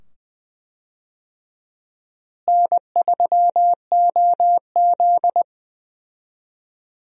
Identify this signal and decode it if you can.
Morse code, heard as a keyed tone, about 20 wpm, 700 Hz. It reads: N3OZ
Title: USA Call Signs - 4 Characters - Repeated 2x - 20wpm